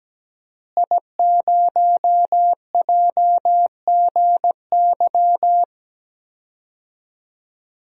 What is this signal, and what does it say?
Morse code, heard as a keyed tone, about 17 wpm, 700 Hz. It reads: I0JGY
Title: DXCC-100 Call Signs - 5 Characters - Repeated 2x - 17wpm